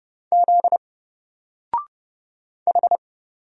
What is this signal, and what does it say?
Morse code, heard as a keyed tone, about 30 wpm, 700 Hz. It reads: Z H
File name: Single-Letter-Number-30wpm